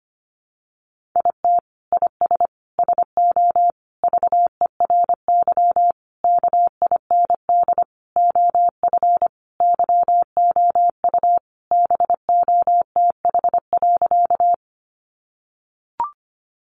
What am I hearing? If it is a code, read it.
Morse code, heard as a keyed tone, about 25 wpm, 700 Hz. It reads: IT IS SO VERY KIND OF YOU BOTH.
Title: Sentences from Top 300 Words 25wpm